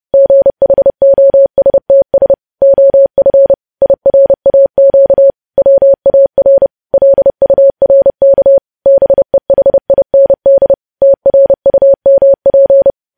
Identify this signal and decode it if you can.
Morse code, heard as a keyed tone, about 30 wpm, 550 Hz. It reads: GHOSTS OF IRAQ WAR LURK BEHIND TRUMP
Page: news headlines in morse code at 30 wpm